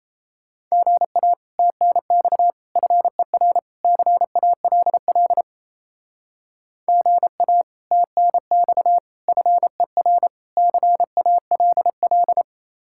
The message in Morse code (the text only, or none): GA TNX FER CALL GA TNX FER CALL